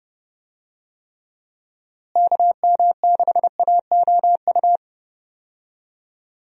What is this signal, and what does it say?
Morse code, heard as a keyed tone, about 30 wpm, 700 Hz. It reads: KM6AOU
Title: USA Call Signs - 6 Characters - Repeated 3x - 30wpm